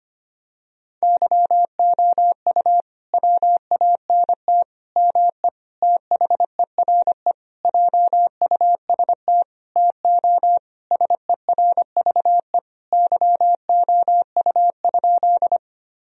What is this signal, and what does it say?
Morse code, heard as a keyed tone, about 25 wpm, 700 Hz. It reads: YOU WANT ME THERE JUST TO SERVE YOU?
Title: Sentences from Top 1000 Words - Review Part 2 - 25wpm